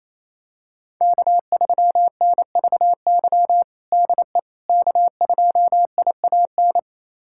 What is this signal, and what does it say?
Morse code, heard as a keyed tone, about 28 wpm, 700 Hz. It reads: K3NVY DE K2IAN
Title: USA Callsign DE Callsign - 5 Characters - 28wpm